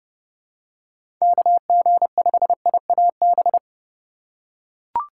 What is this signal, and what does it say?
Morse code, heard as a keyed tone, about 30 wpm, 700 Hz. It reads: KG5IAB
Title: USA Call Signs - 6 Characters - 2x Character Spacing - 30wpm